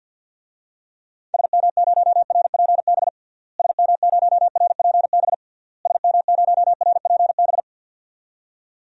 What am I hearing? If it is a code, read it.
Morse code, heard as a keyed tone, about 52 wpm, 700 Hz. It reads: SM0RPB SM0RPB SM0RPB
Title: DXCC-100 Call Signs - 6 Characters - Repeated 3x - 50wpm